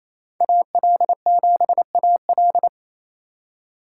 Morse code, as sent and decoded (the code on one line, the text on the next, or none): .- .-.. --... .- .-..
AL7AL